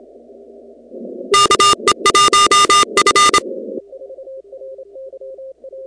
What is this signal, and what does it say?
Morse code, heard as a keyed tone, about 27 wpm, 1210 Hz, with noise. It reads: KE1F
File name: contests / 2019-cw-wpx-kt5j-k5pi- -2330 0-16